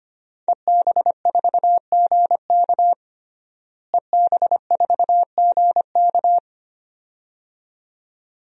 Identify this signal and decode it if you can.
Morse code, heard as a keyed tone, about 25 wpm, 700 Hz. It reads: EB4GK EB4GK
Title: DXCC-100 Call Signs - 5 Characters - Repeated 2x - 25wpm